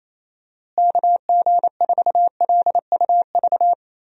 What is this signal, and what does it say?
Morse code, heard as a keyed tone, about 28 wpm, 700 Hz. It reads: KG4LUV
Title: USA Call Signs - 6 Characters - Repeated 3x - 28wpm